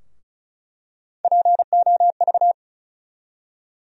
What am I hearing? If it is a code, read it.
Morse code, heard as a keyed tone, about 34 wpm, 700 Hz. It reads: POV
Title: USA Call Sign Postfixes - 3 Letters 35wpm